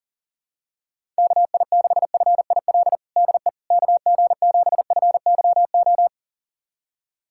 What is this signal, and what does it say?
Morse code, heard as a keyed tone, about 40 wpm, 700 Hz. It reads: KI6FIL DE KG7FYO
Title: USA Callsign DE Callsign 40wpm